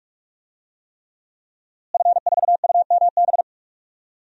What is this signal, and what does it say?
Morse code, heard as a keyed tone, about 48 wpm, 700 Hz. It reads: F4UMB
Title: DXCC-100 Call Signs - 5 Characters - Repeated 3x - 45wpm